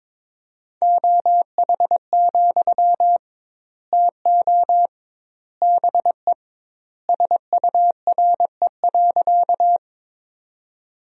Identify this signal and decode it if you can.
Morse code, heard as a keyed tone, about 22 wpm, 700 Hz. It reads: OH, TO BE SURE.
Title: 2x Word-Spacing - Sentences from Top 500 Words - 22wpm